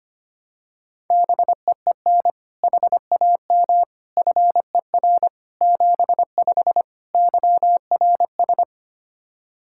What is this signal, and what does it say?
Morse code, heard as a keyed tone, about 25 wpm, 700 Hz. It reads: BEEN HAM FER 75 YRS
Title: Practice QSO - Exchange 3 - More Info - 25wpm